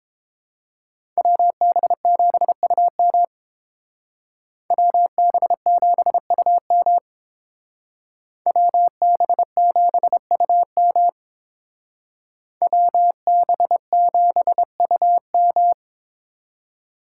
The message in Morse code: WB7UM WB7UM WB7UM WB7UM

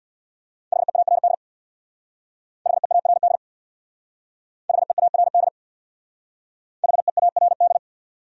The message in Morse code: HEARD HEARD HEARD HEARD

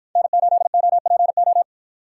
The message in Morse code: N8OPO

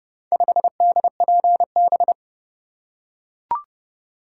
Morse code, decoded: 5DPB

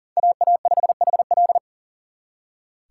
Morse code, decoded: AA5HL

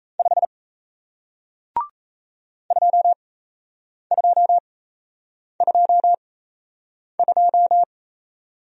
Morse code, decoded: 5 2222

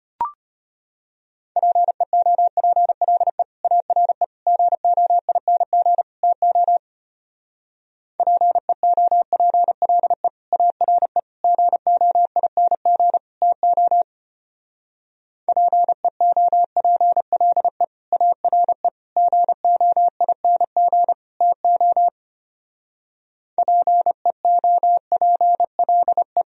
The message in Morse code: PEOPLE ARE GOING TO PEOPLE ARE GOING TO PEOPLE ARE GOING TO PEOPLE

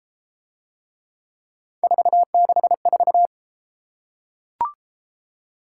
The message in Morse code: 464

